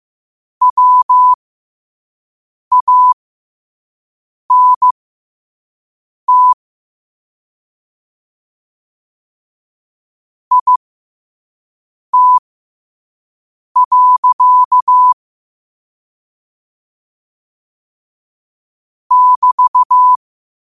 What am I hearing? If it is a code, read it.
Morse code, heard as a keyed tone, about 15 wpm, spaced as for 5 wpm, 1000 Hz. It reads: WANT IT. <BT>
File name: qotdfiles / 5wpm